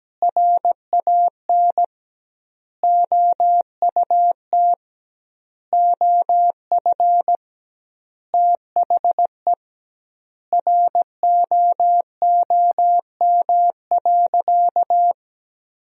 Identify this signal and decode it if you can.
Morse code, heard as a keyed tone, about 17 wpm, 700 Hz. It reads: RAN OUT OF THE ROOM.
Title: 2x Word-Spacing - Sentences from Top 500 Words - 17wpm